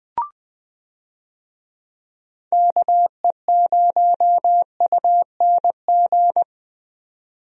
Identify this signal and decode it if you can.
Morse code, heard as a keyed tone, about 21 wpm, 700 Hz. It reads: KE0UNG
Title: USA Call Signs - 6 Characters - 20wpm